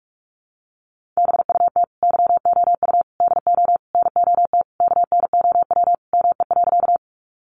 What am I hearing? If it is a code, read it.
Morse code, heard as a keyed tone, about 46 wpm, 700 Hz. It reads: BUT YOU DO NOT KNOW ME.
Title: Sentences from Top 600 Words 45wpm